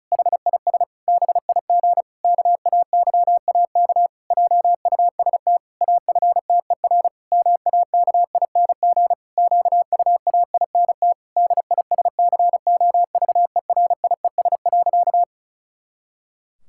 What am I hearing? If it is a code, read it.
Morse code, heard as a keyed tone, about 36 wpm, 700 Hz. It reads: HIS BIG KAYAK JUST AFTER MAKING QUAINT DISCOVERIES.
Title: Pangram Challenge v2 - 35wpm